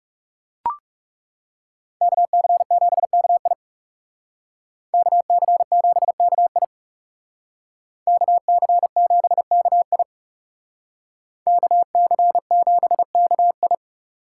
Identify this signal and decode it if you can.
Morse code, heard as a keyed tone, about 39 wpm, 700 Hz. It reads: KC7KI KC7KI KC7KI KC7KI